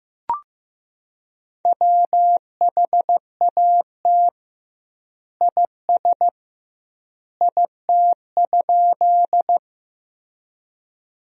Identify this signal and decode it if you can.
Morse code, heard as a keyed tone, about 15 wpm, 700 Hz. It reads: WHAT IS IT?
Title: 2x Word-Spacing - Sentences from Top 100 Words - 15wpm